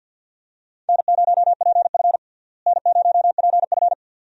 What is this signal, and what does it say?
Morse code, heard as a keyed tone, about 46 wpm, 700 Hz. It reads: N0PF N0PF
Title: USA Call Signs - 4 Characters - Repeated 2x - 50wpm